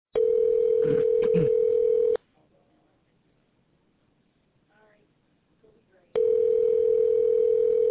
Ringback tone (North America), 440 + 480 Hz.